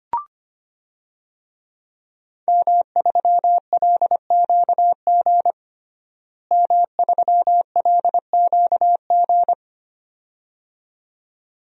Morse code, heard as a keyed tone, about 25 wpm, 700 Hz. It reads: M3LQG M3LQG
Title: DXCC-100 Call Signs - 5 Characters - Repeated 2x - 25wpm